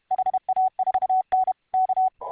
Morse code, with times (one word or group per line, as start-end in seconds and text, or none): 0.1-1.5 HA4N
1.7-2.1 K